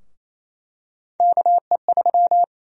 Morse code, heard as a keyed tone, about 28 wpm, 700 Hz. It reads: KE3
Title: USA Call Sign Prefixes - Repeated 2x - 28wpm